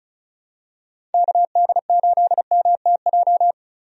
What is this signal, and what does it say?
Morse code, heard as a keyed tone, about 36 wpm, 700 Hz. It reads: KD8MTJ
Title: USA Call Signs - 6 Characters - Repeated 3x - 35wpm